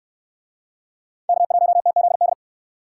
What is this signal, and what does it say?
Morse code, heard as a keyed tone, about 69 wpm, 700 Hz. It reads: K1TZD